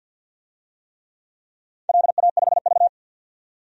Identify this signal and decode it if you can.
Morse code, heard as a keyed tone, about 50 wpm, 700 Hz. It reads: LA5V
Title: DXCC-100 Call Signs - 4 Characters - 50wpm